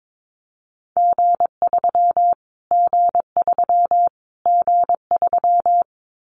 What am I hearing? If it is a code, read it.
Morse code, heard as a keyed tone, about 22 wpm, 700 Hz. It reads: G3 G3 G3
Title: Callsign Prefixes for Top 25 DXCC - Repeated 3x - 22wpm